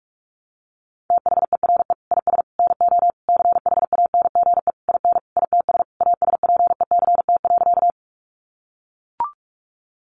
Morse code, heard as a keyed tone, about 44 wpm, 700 Hz. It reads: THERE IS NO CHANGE IN ITS ASPECT.
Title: Sentences from Top 1300 Words 45wpm